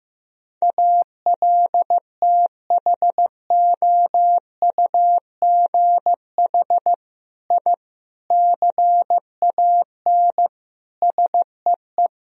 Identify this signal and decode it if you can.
Morse code, heard as a keyed tone, about 15 wpm, 700 Hz. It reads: ALTHOUGH I CAN SEE
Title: Sentences from Top 1000 Words - Review Part 3 - 15wpm